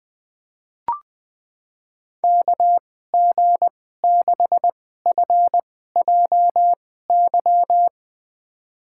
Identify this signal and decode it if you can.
Morse code, heard as a keyed tone, about 20 wpm, spaced as for 14 wpm, 700 Hz. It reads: KG6FJY
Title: USA Call Signs - 6 Characters - 2x Character Spacing - 20wpm